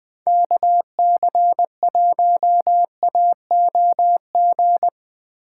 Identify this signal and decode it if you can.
Morse code, heard as a keyed tone, about 20 wpm, 700 Hz. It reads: KC1AOG
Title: Speed Racing - USA 2x3 Call Signs 17wpm